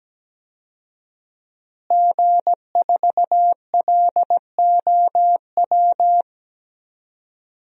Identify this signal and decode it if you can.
Morse code, heard as a keyed tone, about 17 wpm, 700 Hz. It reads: G4LOW
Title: DXCC-100 Call Signs - 5 Characters - Repeated 2x - 17wpm